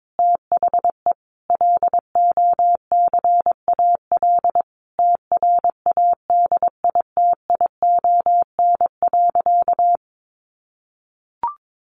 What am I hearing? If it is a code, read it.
Morse code, heard as a keyed tone, about 22 wpm, 700 Hz. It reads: THE LOCAL TRADITION.
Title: Sentences from Top 1500 Words 22wpm